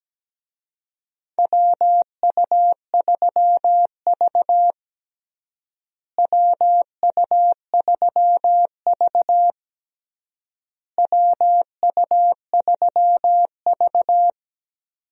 Morse code, heard as a keyed tone, about 17 wpm, 700 Hz. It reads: WU3V WU3V WU3V
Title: USA Call Signs - 4 Characters - Repeated 3x - 17wpm